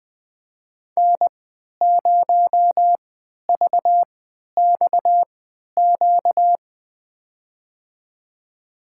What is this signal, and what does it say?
Morse code, heard as a keyed tone, about 20 wpm, spaced as for 11 wpm, 700 Hz. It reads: N0VXQ
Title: USA Call Signs - 5 Characters - 3x Character Spacing - 20wpm